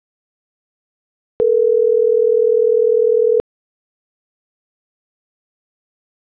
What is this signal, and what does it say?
Ringback tone (North America), 440 + 480 Hz.